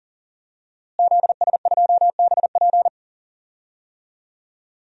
Morse code, heard as a keyed tone, about 40 wpm, 700 Hz. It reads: ZS2BP
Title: DXCC-100 Call Signs - 5 Characters - Repeated 2x - 40wpm